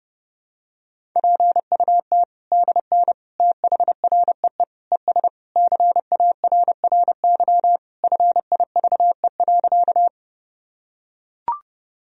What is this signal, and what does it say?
Morse code, heard as a keyed tone, about 32 wpm, 700 Hz. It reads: PUT DN THREE ES CARRY FIVE.
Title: Sentences from Top 400 Words and Abbreviations 30wpm